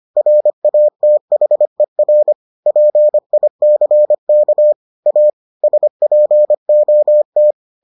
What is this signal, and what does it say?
Morse code, heard as a keyed tone, about 25 wpm, 600 Hz. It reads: RATHER PICK A SPOT